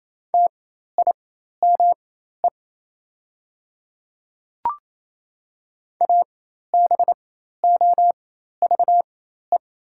Morse code, heard as a keyed tone, about 28 wpm, spaced as for 13 wpm, 700 Hz. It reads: TIME ABOVE
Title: 4x Character Spacing - Top 500 Words - 28wpm